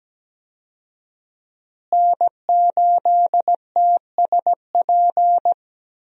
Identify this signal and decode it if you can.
Morse code, heard as a keyed tone, about 17 wpm, 700 Hz. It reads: N8TSP
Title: USA Call Signs - 5 Characters - Repeated 2x - 17wpm